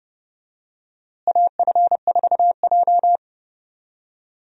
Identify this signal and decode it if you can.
Morse code, heard as a keyed tone, about 30 wpm, 700 Hz. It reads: AF4J
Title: USA Call Signs - 4 Characters - Repeated 2x - 30wpm